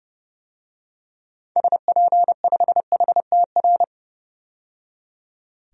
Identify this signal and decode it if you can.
Morse code, heard as a keyed tone, about 30 wpm, 700 Hz. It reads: SP5HTR
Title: DXCC-100 Call Signs - 6 Characters - 30wpm